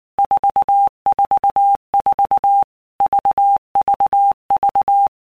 Morse code, heard as a keyed tone, about 19 wpm, 800 Hz. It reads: 444 VVV
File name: Basic Letter Groups By Blocks / Groups@18 / VB46@18